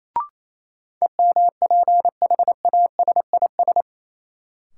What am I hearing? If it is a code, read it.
Morse code, heard as a keyed tone, about 28 wpm, 700 Hz. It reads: EMPHASIS
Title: Sentences from Top 1400 Words - New Word Drill - 28wpm